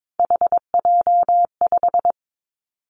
Morse code, heard as a keyed tone, about 22 wpm, 700 Hz. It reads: HJ5
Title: Callsign Prefixes for Top 100 DXCC - 22wpm